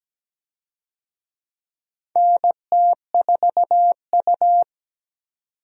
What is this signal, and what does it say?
Morse code, heard as a keyed tone, about 17 wpm, 700 Hz. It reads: NT4U
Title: USA Call Signs - 4 Characters - 17wpm